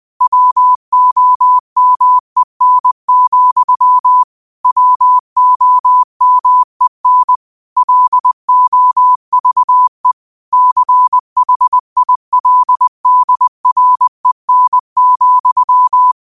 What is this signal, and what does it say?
Morse code, heard as a keyed tone, about 20 wpm, 1000 Hz. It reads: WOMEN, WOMEN LOVE CHILDREN,